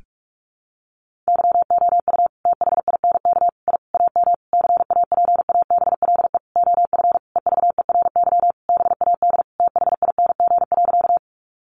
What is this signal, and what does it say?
Morse code, heard as a keyed tone, about 44 wpm, 700 Hz. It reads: YOU THINK I AM CAPABLE OF EVERY BAD THING.
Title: Sentences from Top 1200 Words 45wpm